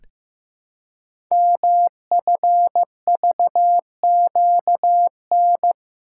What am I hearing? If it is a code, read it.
Morse code, heard as a keyed tone, about 15 wpm, 700 Hz. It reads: MFVQN
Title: Code Groups - 5 Letters - 15wpm